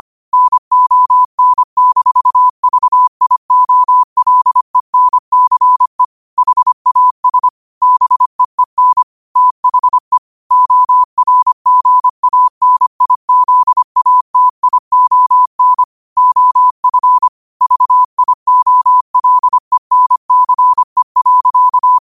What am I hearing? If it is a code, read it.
Morse code, heard as a keyed tone, about 25 wpm, 1000 Hz. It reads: NON-VIOLENCE HAS BEEN THE ORGANIZATION OF VIOLENCE.
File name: qotdfiles / 25wpm